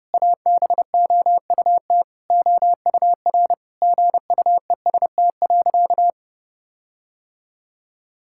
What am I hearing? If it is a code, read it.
Morse code, heard as a keyed tone, about 30 wpm, 700 Hz. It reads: ABOUT OUR GUEST.
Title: Sentences from Top 1600 Words 30wpm